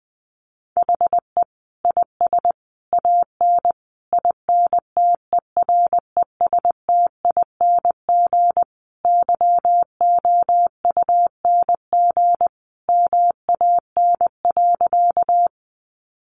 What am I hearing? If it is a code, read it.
Morse code, heard as a keyed tone, about 20 wpm, 700 Hz. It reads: HE IS AN INTERESTING YOUNG MAN.
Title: Sentences from Top 1000 Words 20wpm